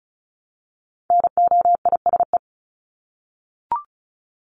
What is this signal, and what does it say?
Morse code, heard as a keyed tone, about 34 wpm, 700 Hz. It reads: NOISE
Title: Top 1501-1750 Words 35wpm